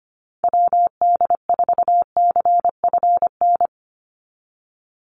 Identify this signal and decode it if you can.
Morse code, heard as a keyed tone, about 25 wpm, 700 Hz. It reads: WD4CFN
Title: NA SOTA Chaser Callsigns 25wpm